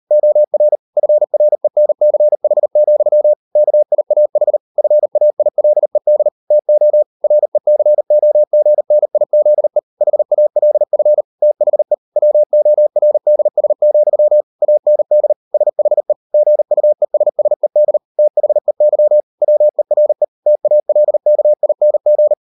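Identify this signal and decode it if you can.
Morse code, heard as a keyed tone, about 38 wpm, 600 Hz. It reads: OR FRENCH, KIAH FAILED TO RECOGNIZE HALF THE WORDS, AND SHE GUESSED THEY WERE TALKING
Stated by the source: Morse code practice at 38 WPM, sending Ken Follett's novel Never in English, from CW med Gnister.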